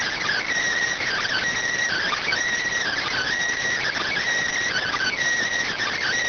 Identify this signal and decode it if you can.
Emergency Alert System (SAME) tones, heard as harsh buzzing bursts. It is heard from 1.0 to 5.1 s.